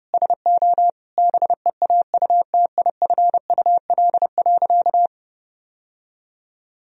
Morse code, heard as a keyed tone, about 32 wpm, 700 Hz. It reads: SO BEAUTIFUL.
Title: Sentences from Top 1000 Words - Review Part 3 - 30wpm